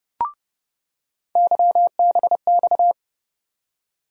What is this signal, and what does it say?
Morse code, heard as a keyed tone, about 30 wpm, 700 Hz. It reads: YBX
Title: USA Call Sign Postfixes - 3 Letters 30wpm